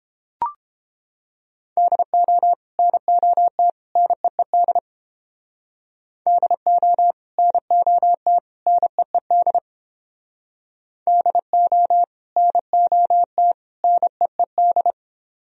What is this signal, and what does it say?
Morse code, heard as a keyed tone, about 30 wpm, 700 Hz. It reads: DO NOT NEED DO NOT NEED DO NOT NEED